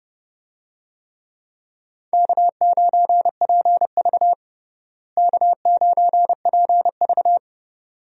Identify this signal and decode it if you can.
Morse code, heard as a keyed tone, about 30 wpm, 700 Hz. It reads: K9PV K9PV
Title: USA Call Signs - 4 Characters - Repeated 2x - 30wpm